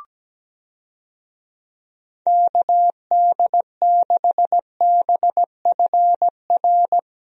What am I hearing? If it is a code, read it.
Morse code, heard as a keyed tone, about 17 wpm, 700 Hz. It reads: KD6BFR
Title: USA Call Signs - 6 Characters - Repeated 2x - 17wpm